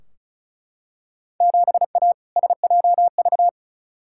Morse code, heard as a keyed tone, about 36 wpm, 690 Hz. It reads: 7A SJV
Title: Field Day Exchanges Encore - 35wpm